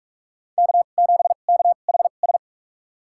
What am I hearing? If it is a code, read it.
Morse code, heard as a keyed tone, about 47 wpm, spaced as for 33 wpm, 700 Hz. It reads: K7KHS
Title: USA Call Signs - 5 Characters - 2x Character Spacing - 45wpm